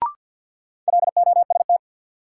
Morse code, heard as a keyed tone, about 52 wpm, 700 Hz. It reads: LOST
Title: Sentences from Top 1400 Words - New Word Drill Alt - 50wpm